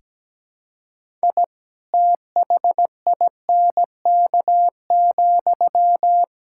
Morse code, heard as a keyed tone, about 17 wpm, 700 Hz. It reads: I THINK,
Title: Sentences from Top 1000 Words - Review Part 5 - 17wpm